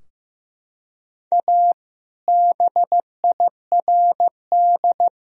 Morse code, heard as a keyed tone, about 16 wpm, 700 Hz. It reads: A BIRD
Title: Sentences from Top 1000 Words - Review Part 2 - 15wpm